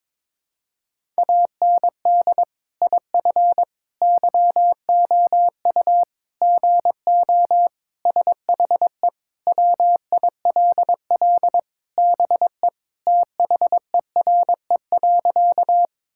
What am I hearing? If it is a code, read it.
Morse code, heard as a keyed tone, about 22 wpm, 700 Hz. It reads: AND IF YOU GO SHE WILL BE THERE.
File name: practice / 22wpm / Sentences with 3-Word Phrases - 22wpm